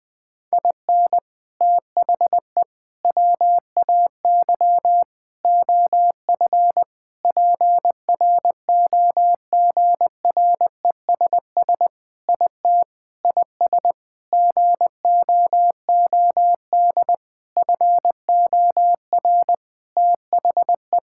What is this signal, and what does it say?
Morse code, heard as a keyed tone, about 21 wpm, 700 Hz. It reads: IN THE WAY OF PROGRESS IT IS GOOD FOR THE
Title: Sentences with 4-Word Phrases - 20wpm